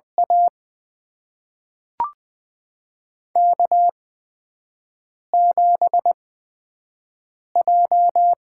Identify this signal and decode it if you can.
Morse code, heard as a keyed tone, about 20 wpm, spaced as for 5 wpm, 700 Hz. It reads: A K7J
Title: USA Call Signs - 4 Characters - 8x Character Spacing - 20wpm